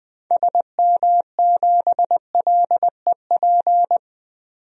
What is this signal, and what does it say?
Morse code, heard as a keyed tone, about 20 wpm, 700 Hz. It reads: SM7LEP
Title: DXCC-100 Call Signs - 6 Characters - 20wpm